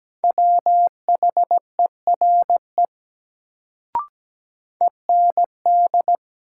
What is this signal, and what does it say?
Morse code, heard as a keyed tone, about 17 wpm, 700 Hz. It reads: WHERE END
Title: Sentences from Top 200 Words - New Word Drill - 17wpm